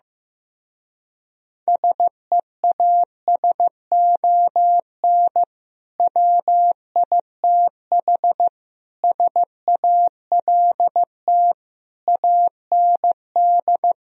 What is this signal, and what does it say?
Morse code, heard as a keyed tone, about 15 wpm, 700 Hz. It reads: SEASON WITH SALT AND
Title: Speed-Racing - Sets of 5 Words 15wpm